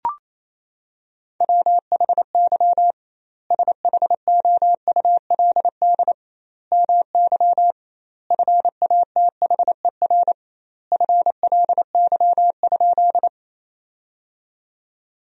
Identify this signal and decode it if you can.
Morse code, heard as a keyed tone, about 28 wpm, 700 Hz. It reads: WHY SHOULD MY FATHER FLY?
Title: 2x Word-Spacing - Sentences from Top 500 Words - 28wpm